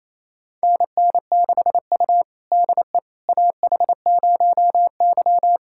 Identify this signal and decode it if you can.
Morse code, heard as a keyed tone, about 28 wpm, 700 Hz. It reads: NN6U DE AH0Y
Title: USA Callsign DE Callsign - 4 Characters - 28wpm